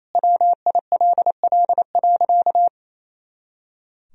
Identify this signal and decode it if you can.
Morse code, heard as a keyed tone, about 28 wpm, 700 Hz. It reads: WILL.
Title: Sentences from Top 500 Words and Abbreviations 28wpm